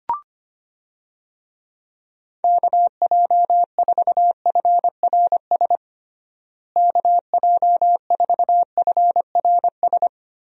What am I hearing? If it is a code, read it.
Morse code, heard as a keyed tone, about 25 wpm, 700 Hz. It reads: KJ4FRS KJ4FRS
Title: USA Call Signs - 6 Characters - Repeated 3x - 25wpm